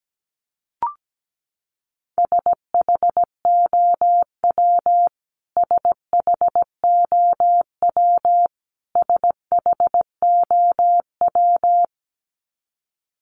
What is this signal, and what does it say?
Morse code, heard as a keyed tone, about 17 wpm, 700 Hz. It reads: SHOW SHOW SHOW
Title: Top 500 Words Repeated 17wpm